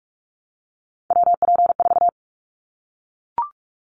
Morse code, heard as a keyed tone, about 46 wpm, 700 Hz. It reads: WP4
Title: Callsign Prefixes for Top 25 DXCC - 45wpm